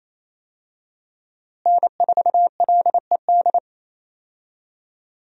Morse code, heard as a keyed tone, about 28 wpm, 700 Hz. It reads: N4LED
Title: USA Call Signs - 5 Characters - Repeated 2x - 28wpm